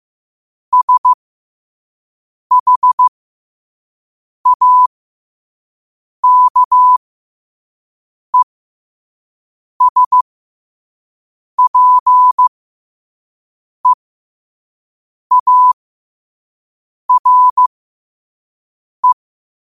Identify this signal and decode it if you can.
Morse code, heard as a keyed tone, about 15 wpm, spaced as for 5 wpm, 1000 Hz. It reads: SHAKESPEARE
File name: qotdfiles / 5wpm